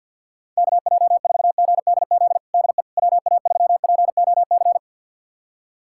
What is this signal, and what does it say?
Morse code, heard as a keyed tone, about 46 wpm, 700 Hz. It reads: KJ4GDZ DE WA3POC
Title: USA Callsign DE Callsign 50wpm